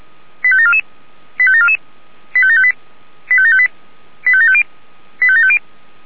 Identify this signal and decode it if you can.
Five-tone selective call (ZVEI), heard as quick runs of five tones, tones of about 70 ms. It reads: 85744 85744 85758 85758 85750 85750
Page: SDR/ZVEI